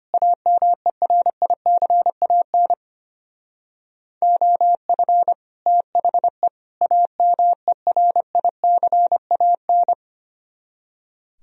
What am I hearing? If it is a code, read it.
Morse code, heard as a keyed tone, about 32 wpm, 700 Hz. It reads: AMERICAN OF THE AMERICAN